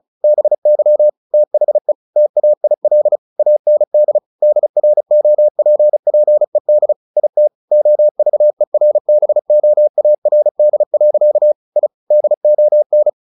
Morse code, heard as a keyed tone, about 34 wpm, 600 Hz. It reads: BY THE TAIL AND DROPPED IT OVERBOARD. I DON
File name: the old man and the sea / the old man and the sea 1 35wpm